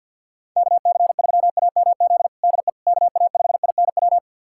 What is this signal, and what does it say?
Morse code, heard as a keyed tone, about 51 wpm, 690 Hz. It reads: KC3AMZ DE KA5INW
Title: USA Callsign DE Callsign - 6 Characters - 50wpm